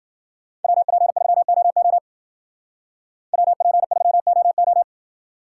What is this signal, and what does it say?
Morse code, heard as a keyed tone, about 50 wpm, 700 Hz. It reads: WP3OO WP3OO
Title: Speed Racing - USA 2x2 Call Signs 40wpm